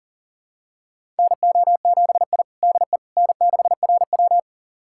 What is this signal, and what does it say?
Morse code, heard as a keyed tone, about 40 wpm, 700 Hz. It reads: NO7I DE N6RW
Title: USA Callsign DE Callsign - 4 Characters - 40wpm